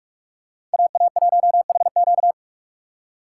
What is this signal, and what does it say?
Morse code, heard as a keyed tone, about 46 wpm, 700 Hz. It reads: AA1HQ
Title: USA Call Signs - 5 Characters - 2x Character Spacing - 45wpm